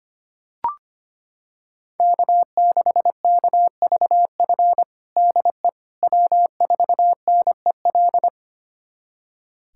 Morse code, heard as a keyed tone, about 25 wpm, 700 Hz. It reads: K6KVF DE W4NEL
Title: USA Callsign DE Callsign - 5 Characters - 25wpm